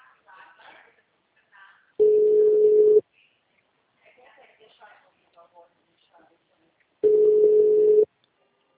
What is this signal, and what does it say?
Ringback tone (Europe, 425 Hz), with speech.
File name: out- -2004- - - .41294